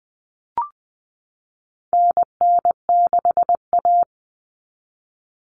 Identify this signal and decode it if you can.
Morse code, heard as a keyed tone, about 20 wpm, 700 Hz. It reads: NN6A
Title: Common Cut Numbers - 4-Digit 20wpm